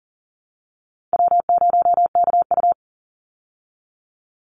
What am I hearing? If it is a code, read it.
Morse code, heard as a keyed tone, about 40 wpm, 700 Hz. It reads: W0KU
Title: Easy Call Signs Repeated 2x - 40wpm